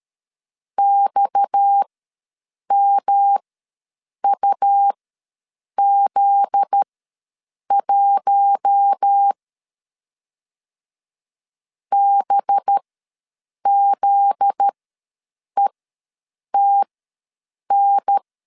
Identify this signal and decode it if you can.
Morse code, heard as a keyed tone, about 13 wpm, spaced as for 7 wpm, 800 Hz. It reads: XMUZ1 BZETN
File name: 7 GF9Z ws 12wpm